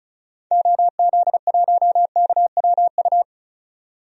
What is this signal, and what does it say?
Morse code, heard as a keyed tone, about 36 wpm, 700 Hz. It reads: OZ1KWU
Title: DXCC-100 Call Signs - 6 Characters - 35wpm